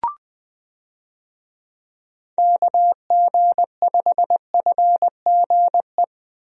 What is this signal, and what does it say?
Morse code, heard as a keyed tone, about 20 wpm, 700 Hz. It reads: KG5FGE
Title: USA Call Signs - 6 Characters - Repeated 2x - 20wpm